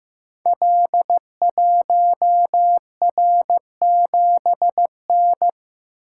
Morse code, heard as a keyed tone, about 15 wpm, 700 Hz. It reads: L1R7N
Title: Code Groups - 5 Letter:Numbers - 15wpm